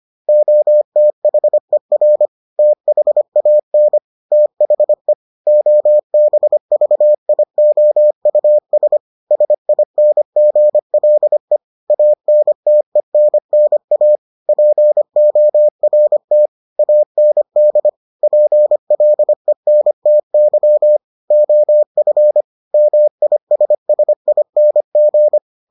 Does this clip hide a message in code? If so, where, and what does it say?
Yes: Morse code from 0.3 to 25.4 s, heard as a keyed tone, about 25 wpm, 600 Hz. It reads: OTHER THAN THE OBVIOUS SINGLE ANTENNA PORT AND PLENTY OF MISSING